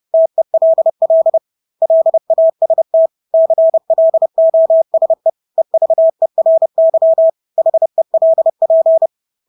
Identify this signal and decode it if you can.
Morse code, heard as a keyed tone, about 30 wpm, 650 Hz. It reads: TELL LAST CLOSE EVERY HELP
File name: Engelske ord 30-30 wpm